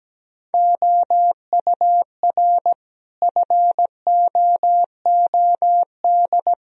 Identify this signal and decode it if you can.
Morse code, heard as a keyed tone, about 17 wpm, 700 Hz. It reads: OUR FOOD
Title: Sentences from Top 1000 Words - Review Part 1 - 17wpm